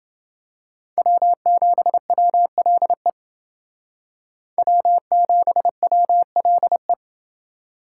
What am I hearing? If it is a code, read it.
Morse code, heard as a keyed tone, about 29 wpm, 700 Hz. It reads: W7WLE W7WLE